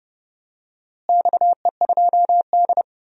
Morse code, heard as a keyed tone, about 30 wpm, 700 Hz. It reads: XE2D
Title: DXCC-100 Call Signs - 4 Characters - Repeated 3x - 30wpm